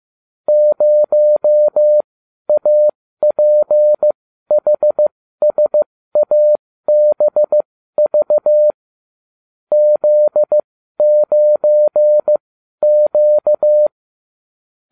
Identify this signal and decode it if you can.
Morse code, heard as a keyed tone, about 15 wpm, spaced as for 12 wpm, 600 Hz. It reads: 0APHSABV Z9Q